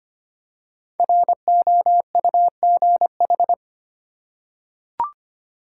Morse code, heard as a keyed tone, about 25 wpm, 700 Hz. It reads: ROUGH
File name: 5 Letter Words - First 3 Letters Given - 25wpm